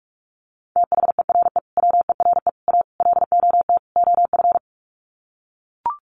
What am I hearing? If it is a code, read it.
Morse code, heard as a keyed tone, about 44 wpm, 700 Hz. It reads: THERE WERE A LOT OF
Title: Sets of 5 Words 45wpm